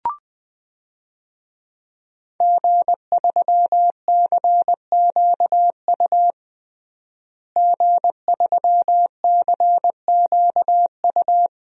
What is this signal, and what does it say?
Morse code, heard as a keyed tone, about 20 wpm, 700 Hz. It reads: G3CQU G3CQU
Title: DXCC-100 Call Signs - 5 Characters - Repeated 2x - 20wpm